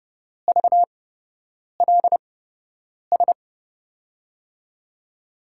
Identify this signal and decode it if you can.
Morse code, heard as a keyed tone, about 32 wpm, spaced as for 8 wpm, 700 Hz. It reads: VLS
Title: USA Call Signs - 6 Characters - 8x Character Spacing - 30wpm